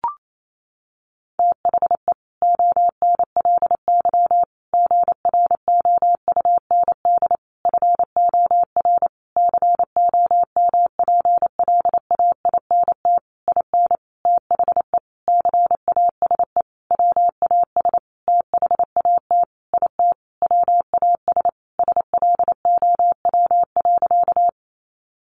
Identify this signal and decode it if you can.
Morse code, heard as a keyed tone, about 28 wpm, 700 Hz. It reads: THE ONLY GROUND FOR COMPLAINT IN THE CASE WAS THAT IT WAS SLOW.
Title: Sentences from Top 1400 Words 28wpm